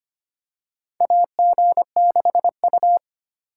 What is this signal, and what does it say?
Morse code, heard as a keyed tone, about 25 wpm, 700 Hz. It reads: AG6U
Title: USA Call Signs - 4 Characters - 3x Character Spacing - 25wpm